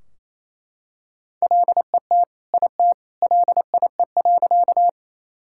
Morse code, heard as a keyed tone, about 28 wpm, 700 Hz. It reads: LET IT LIE.
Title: Sentences from Top 1600 Words 28wpm